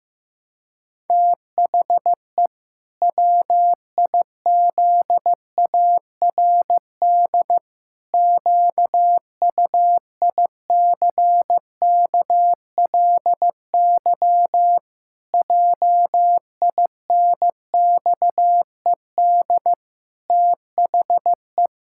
Morse code, heard as a keyed tone, about 15 wpm, 700 Hz. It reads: THE WIZARD QUICKLY JINXED THE
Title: Pangram Challenge v2 - 15wpm